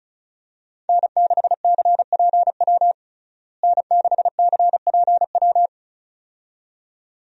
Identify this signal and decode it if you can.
Morse code, heard as a keyed tone, about 34 wpm, 700 Hz. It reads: N6CPW N6CPW
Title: USA Call Signs - 5 Characters - Repeated 2x - 35wpm